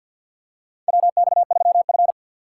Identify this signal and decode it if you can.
Morse code, heard as a keyed tone, about 51 wpm, 700 Hz. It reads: WX3F
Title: USA Call Signs - 4 Characters - 3x Character Spacing - 50wpm